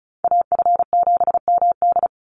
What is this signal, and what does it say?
Morse code, heard as a keyed tone, about 34 wpm, 700 Hz. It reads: AF7MD
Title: NA SOTA Chaser Callsigns 35wpm